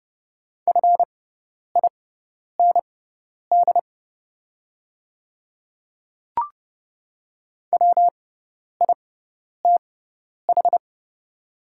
Morse code, heard as a keyed tone, about 30 wpm, spaced as for 10 wpm, 700 Hz. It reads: FIND WITH
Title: 6x Character Spacing - Top 100 Words - 30wpm